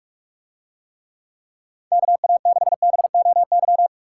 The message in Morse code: KA6BOY

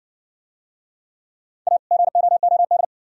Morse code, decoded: A GOOD

